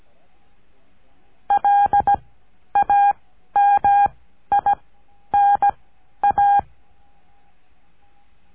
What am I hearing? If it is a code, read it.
Morse code, heard as a keyed tone, about 17 wpm, spaced as for 11 wpm, 810 Hz. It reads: LAMINA
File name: cursos / CURSO DE TELEGRAFIA / Telegrafia - Trilha 14